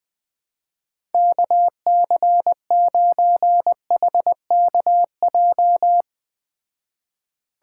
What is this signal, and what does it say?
Morse code, heard as a keyed tone, about 20 wpm, 700 Hz. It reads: KC9HKJ
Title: Speed Racing - USA Call Signs 17wpm